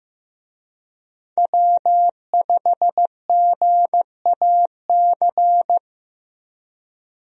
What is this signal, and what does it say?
Morse code, heard as a keyed tone, about 15 wpm, 700 Hz. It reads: W5GAC
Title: USA Call Signs - 5 Characters - Repeated 2x - 15wpm